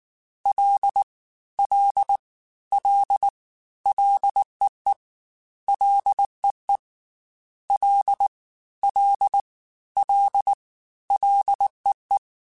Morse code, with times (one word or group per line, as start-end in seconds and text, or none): 0.5-1.0 L
1.6-2.2 L
2.7-3.3 L
3.9-4.9 LEE
5.7-6.8 LEE
7.7-8.3 L
8.8-9.4 L
10.0-10.5 L
11.1-12.2 LEE